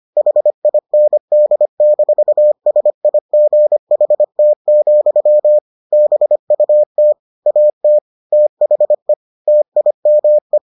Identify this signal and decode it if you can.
Morse code, heard as a keyed tone, about 25 wpm, 600 Hz. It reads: HIND-SIGHT, BUT AT THE TIME